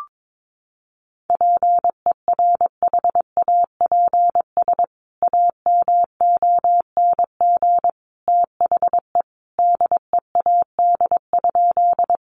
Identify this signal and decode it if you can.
Morse code, heard as a keyed tone, about 22 wpm, 700 Hz. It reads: PERHAPS AMONG THE DEAD?
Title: Sentences from Top 1500 Words 22wpm